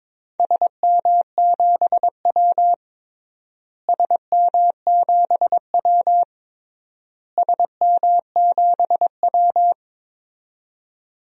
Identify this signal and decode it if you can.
Morse code, heard as a keyed tone, about 22 wpm, 700 Hz. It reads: SM7W SM7W SM7W
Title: DXCC-100 Call Signs - 4 Characters - Repeated 3x - 22wpm